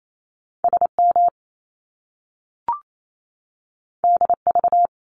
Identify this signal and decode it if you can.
Morse code, heard as a keyed tone, about 28 wpm, 700 Hz. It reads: SM DV
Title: Top Two Letter Combinations in Words 28wpm